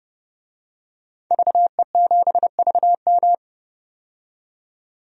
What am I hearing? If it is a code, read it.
Morse code, heard as a keyed tone, about 30 wpm, 700 Hz. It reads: VE7VM
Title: DXCC-100 Call Signs - 5 Characters - 30wpm